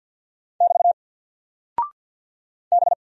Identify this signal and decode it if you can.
Morse code, heard as a keyed tone, about 49 wpm, 700 Hz. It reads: <BT> B
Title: Single Character (Letter:Number:Punctuation:Prosign) Alt 50wpm